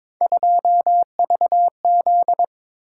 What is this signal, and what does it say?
Morse code, heard as a keyed tone, about 22 wpm, 700 Hz. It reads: 2VZ
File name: practice / 22wpm / Three Letter-Number Combinations - 22wpm